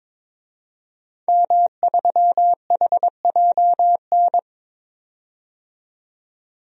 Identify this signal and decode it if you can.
Morse code, heard as a keyed tone, about 22 wpm, 700 Hz. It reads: M3HJN
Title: DXCC-100 Call Signs - 5 Characters - 22wpm